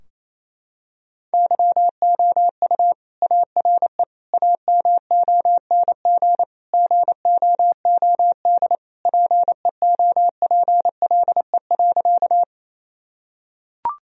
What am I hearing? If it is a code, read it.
Morse code, heard as a keyed tone, about 28 wpm, 700 Hz. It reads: YOU ARE AMONG GOOD PEOPLE.
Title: Sentences from Top 1000 Words - Review Part 3 - 28wpm